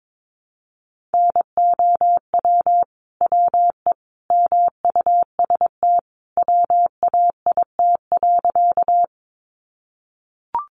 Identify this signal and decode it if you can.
Morse code, heard as a keyed tone, about 22 wpm, 700 Hz. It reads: NOW WE MUST WAIT.
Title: Sentences from Top 500 Words 22wpm